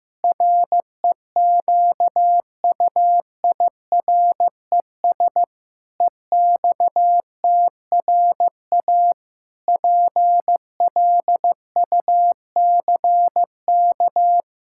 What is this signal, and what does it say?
Morse code, heard as a keyed tone, about 15 wpm, 700 Hz. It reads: REQUIRES EXTRA PLUCK
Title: Pangram Challenge v2 - 15wpm